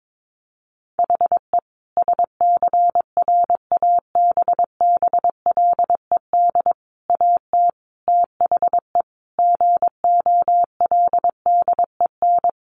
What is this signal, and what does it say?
Morse code, heard as a keyed tone, about 22 wpm, 700 Hz. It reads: HE SCRABBLED AT THE GOLDEN
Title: A Sound Of Thunder 22wpm